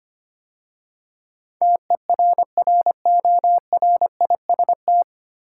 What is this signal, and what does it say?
Morse code, heard as a keyed tone, about 25 wpm, 700 Hz. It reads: TERRORIST
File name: practice / 25wpm / 9 Letter Words - First 3 Letters Given - 25wpm